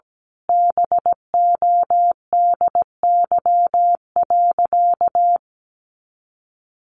Morse code, heard as a keyed tone, about 17 wpm, 700 Hz. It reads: BODY.
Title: A Sound Of Thunder 17wpm